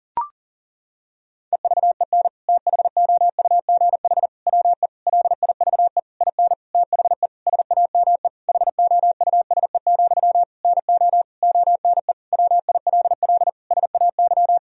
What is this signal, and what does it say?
Morse code, heard as a keyed tone, about 37 wpm, 700 Hz. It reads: EVEN THOUGH WE LIVE IN THE SAME HOUSE, NO ONE WILL SAY
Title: Sentences from Top 1000 Words - Review Part 2 - 40wpm